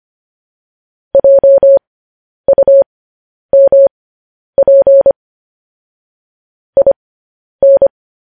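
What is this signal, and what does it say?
Morse code, heard as a keyed tone, about 25 wpm, spaced as for 10 wpm, 550 Hz. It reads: JUMP IN
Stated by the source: Morse code news headlines sent as a reading transcription, preceded by vvv.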